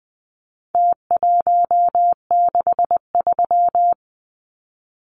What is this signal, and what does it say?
Morse code, heard as a keyed tone, about 20 wpm, 700 Hz. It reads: T163
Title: Numbers - Contest Style - 4-Digit 20wpm